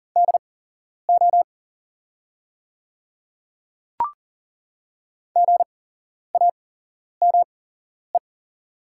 Morse code, heard as a keyed tone, about 40 wpm, spaced as for 11 wpm, 690 Hz. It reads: DO GAME
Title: 8x Character Spacing - Top 500 Words - 40wpm